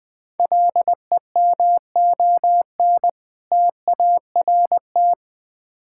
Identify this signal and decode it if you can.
Morse code, heard as a keyed tone, about 20 wpm, 700 Hz. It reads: LEMON TART
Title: Difficult Words with Hints - 20wpm